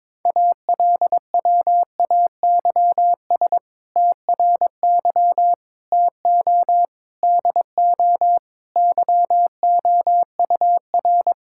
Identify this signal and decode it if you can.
Morse code, heard as a keyed tone, about 22 wpm, 700 Hz. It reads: ALWAYS TRY TO DO YOUR
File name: practice / 22wpm / Sentences with 2-Word Phrases - 22wpm